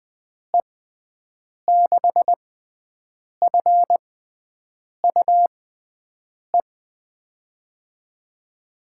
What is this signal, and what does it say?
Morse code, heard as a keyed tone, about 20 wpm, spaced as for 7 wpm, 700 Hz. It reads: E6FUE
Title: USA Call Signs - 6 Characters - 6x Character Spacing - 20wpm